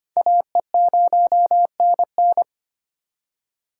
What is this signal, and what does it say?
Morse code, heard as a keyed tone, about 34 wpm, spaced as for 30 wpm, 700 Hz. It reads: AE0NN
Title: Speed Racing - USA Call Signs 25wpm